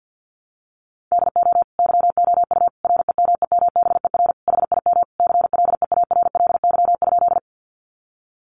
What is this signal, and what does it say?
Morse code, heard as a keyed tone, about 48 wpm, 700 Hz. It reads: DO YOU REMEMBER HIM CLEARLY?
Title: Sentences from Top 1300 Words 50wpm